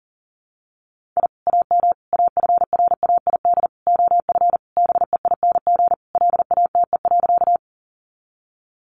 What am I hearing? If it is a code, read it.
Morse code, heard as a keyed tone, about 40 wpm, 700 Hz. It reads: I AM AFRAID OF BEING LATE.
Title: Sentences from Top 1500 Words 40wpm